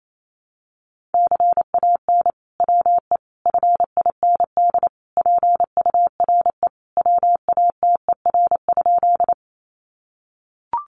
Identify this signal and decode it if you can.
Morse code, heard as a keyed tone, about 28 wpm, 700 Hz. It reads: CAN WE FIND PURE WATER?
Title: Sentences from Top 1500 Words 28wpm